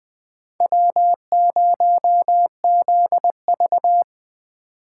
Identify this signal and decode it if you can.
Morse code, heard as a keyed tone, about 23 wpm, 700 Hz. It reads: W0ZV
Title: Speed Racing - USA 1x2 Call Signs 20wpm